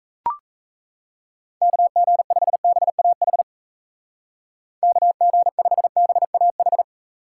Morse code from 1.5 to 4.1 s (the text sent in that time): KG5BAH